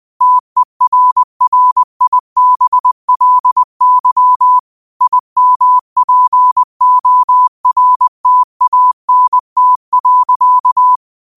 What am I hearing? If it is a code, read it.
Morse code, heard as a keyed tone, about 20 wpm, 1000 Hz. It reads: TERRIBLY IMPORTANT.
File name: qotdfiles / 20wpm